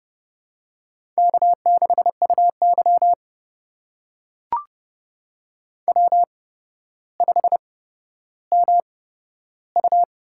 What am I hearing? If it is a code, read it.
Morse code, heard as a keyed tone, about 30 wpm, spaced as for 13 wpm, 700 Hz. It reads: K6UY W5MU
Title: USA Call Signs - 4 Characters - 8x Character Spacing - 30wpm